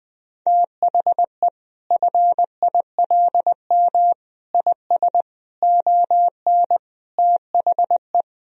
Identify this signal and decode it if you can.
Morse code, heard as a keyed tone, about 21 wpm, 700 Hz. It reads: THE FILM IS ON THE
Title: Sentences from Top 1000 Words - Review Part 3 - 20wpm